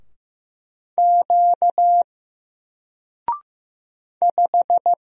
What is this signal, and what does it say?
Morse code, heard as a keyed tone, about 15 wpm, 700 Hz. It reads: Q 5
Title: Single Character (Letter:Number:Punctuation:Prosign) Alt 15wpm